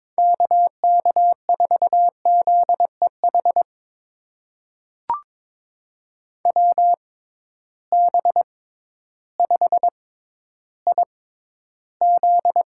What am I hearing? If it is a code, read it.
Morse code, heard as a keyed tone, about 22 wpm, spaced as for 12 wpm, 700 Hz. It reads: KK4ZEH WB5IZ
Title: USA Call Signs - 6 Characters - 6x Character Spacing - 22wpm